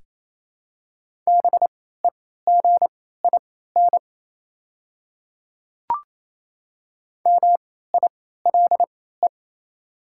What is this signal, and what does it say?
Morse code, heard as a keyed tone, about 27 wpm, spaced as for 16 wpm, 700 Hz. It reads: BEGIN MILE
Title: 3x Character Spacing - Top 500 Words - 28wpm